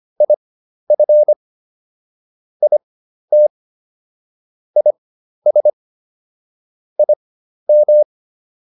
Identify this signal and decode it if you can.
Morse code, heard as a keyed tone, about 25 wpm, spaced as for 12 wpm, 600 Hz. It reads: IF IT IS IM